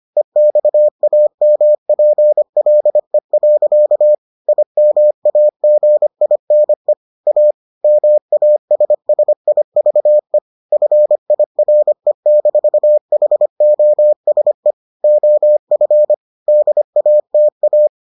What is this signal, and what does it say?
Morse code, heard as a keyed tone, about 25 wpm, 600 Hz. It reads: EXAMPLE. IMAGINE A MASSIVE FIRE-HOSE OF DATA